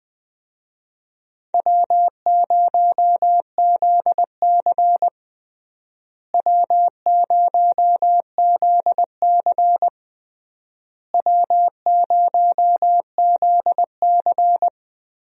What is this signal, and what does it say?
Morse code, heard as a keyed tone, about 21 wpm, 700 Hz. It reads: W0ZC W0ZC W0ZC
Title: USA Call Signs - 4 Characters - Repeated 3x - 20wpm